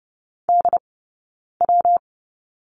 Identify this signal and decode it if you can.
Morse code, heard as a keyed tone, about 30 wpm, 700 Hz. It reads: D W
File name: Lesson 20d - Number 3 - Two Characters 30wpm